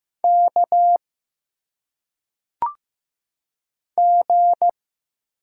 Morse code, heard as a keyed tone, about 15 wpm, 700 Hz. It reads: K G
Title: Single Letters - Phonetic Pronunciation 15wpm